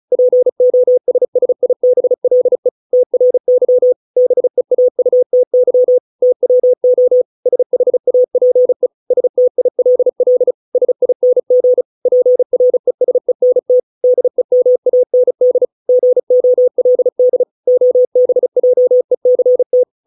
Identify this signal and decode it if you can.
Morse code, heard as a keyed tone, about 34 wpm, 500 Hz. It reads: POSSIBLE TRY BEAUTY TWO SHAPE STILL SING PRESENT DEMAND GOLD OBJECT